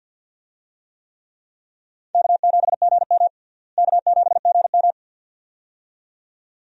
Morse code, heard as a keyed tone, about 48 wpm, 700 Hz. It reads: K7GM K7GM
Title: USA Call Signs - 4 Characters - Repeated 2x - 50wpm